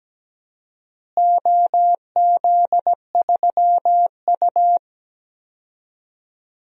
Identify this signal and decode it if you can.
Morse code, heard as a keyed tone, about 17 wpm, 700 Hz. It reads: OZ3U
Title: DXCC-100 Call Signs - 4 Characters - Repeated 3x - 17wpm